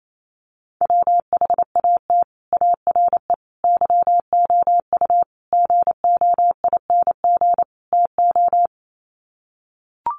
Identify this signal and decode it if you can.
Morse code, heard as a keyed tone, about 28 wpm, 700 Hz. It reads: WHAT ARE YOU GOING TO
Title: Sets of 5 Words 28wpm